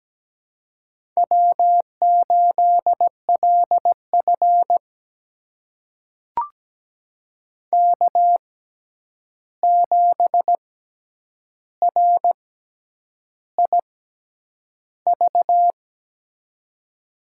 Morse code, heard as a keyed tone, about 17 wpm, spaced as for 8 wpm, 700 Hz. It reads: W8LF K7RIV
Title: USA Call Signs - 6x Character Spacing - 17wpm